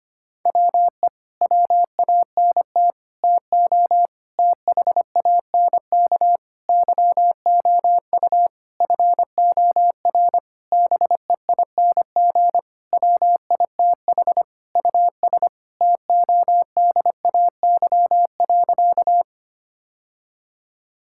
Morse code, heard as a keyed tone, about 25 wpm, 700 Hz. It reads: WE WANT TO THANK YOU FOR BEING WITH US TODAY.
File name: practice / 25wpm / Sentences with 9-Word Phrases - 25wpm